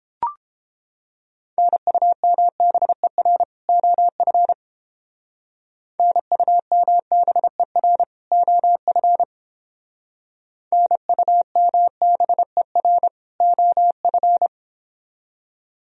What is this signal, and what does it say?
Morse code, heard as a keyed tone, about 30 wpm, 700 Hz. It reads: NUMBER OF NUMBER OF NUMBER OF